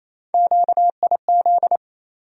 Morse code, heard as a keyed tone, about 28 wpm, 700 Hz. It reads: QIZ
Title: USA Call Sign Postfixes - 3 Letters 28wpm